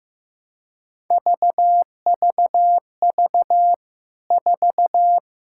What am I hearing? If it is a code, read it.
Morse code, heard as a keyed tone, about 15 wpm, 700 Hz. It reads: VVV 4
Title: K1USN SST QSO Practice 15wpm